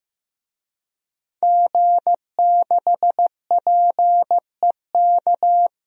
Morse code, heard as a keyed tone, about 15 wpm, 700 Hz. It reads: G6PEK
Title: DXCC-100 Call Signs - 5 Characters - 15wpm